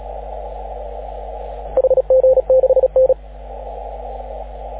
Morse code, heard as a keyed tone, about 34 wpm, 530 Hz. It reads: HG6N
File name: AUDIO2019 / HG6N CW SSB MIX